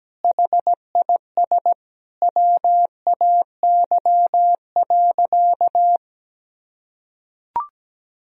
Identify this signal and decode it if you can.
Morse code, heard as a keyed tone, about 17 wpm, 700 Hz. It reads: HIS WAY.
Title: Sentences from Top 300 Words and Abbreviations 17wpm